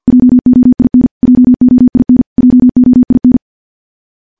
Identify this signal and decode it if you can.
Morse code, heard as a keyed tone, about 50 wpm, 250 Hz. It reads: JOIN JOIN JOIN
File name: morse / morse-training-50WPM